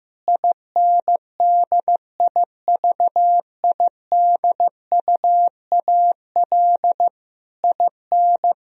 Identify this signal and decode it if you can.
Morse code, heard as a keyed tone, about 15 wpm, 700 Hz. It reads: INDIVIDUAL IN
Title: Sentences from Top 1000 Words - Review Part 4 - 15wpm